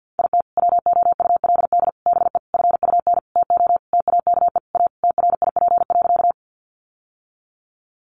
Morse code, heard as a keyed tone, about 50 wpm, 700 Hz. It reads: IT WOULD BE FUN TO TAKE A TRIP.
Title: Sentences from Top 1300 Words 50wpm